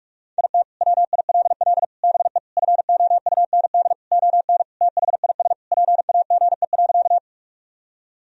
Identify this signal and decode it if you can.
Morse code, heard as a keyed tone, about 46 wpm, 700 Hz. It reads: IT WILL BE FOUND ON THIS PAGE.
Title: Sentences from Top 1000 Words - Review Part 1 - 45wpm